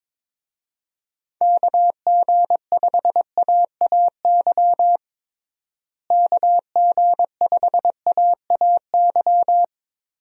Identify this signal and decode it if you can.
Morse code, heard as a keyed tone, about 21 wpm, 700 Hz. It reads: KG5AAY KG5AAY
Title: USA Call Signs - 6 Characters - Repeated 3x - 22wpm